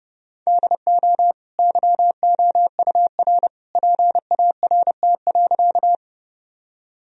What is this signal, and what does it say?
Morse code, heard as a keyed tone, about 30 wpm, 700 Hz. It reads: DO YOUR PART.
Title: Sentences from Top 1000 Words - Review Part 1 - 30wpm